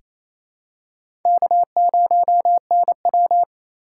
Morse code, heard as a keyed tone, about 28 wpm, 700 Hz. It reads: K0NW
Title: K1USN SST QSO Practice - 28wpm